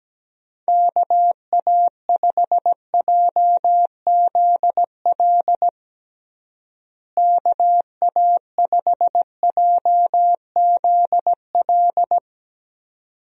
Morse code, heard as a keyed tone, about 17 wpm, 700 Hz. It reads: KA5JZL KA5JZL
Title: USA Call Signs - 6 Characters - Repeated 2x - 17wpm